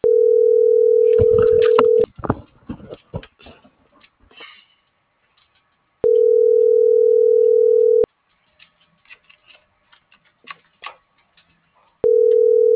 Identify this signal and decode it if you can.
Ringback tone (North America), 440 + 480 Hz.